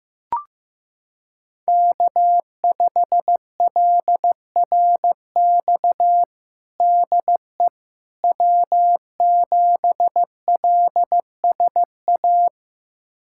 Morse code, heard as a keyed tone, about 15 wpm, 700 Hz. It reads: K5LRX DE W7LSA
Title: USA Callsign DE Callsign 15wpm